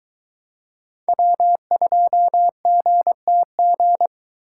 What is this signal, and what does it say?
Morse code, heard as a keyed tone, about 21 wpm, 700 Hz. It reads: W2GTG